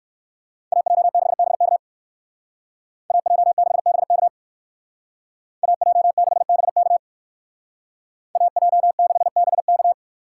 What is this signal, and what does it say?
Morse code, heard as a keyed tone, about 58 wpm, 700 Hz. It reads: AJ6BK AJ6BK AJ6BK AJ6BK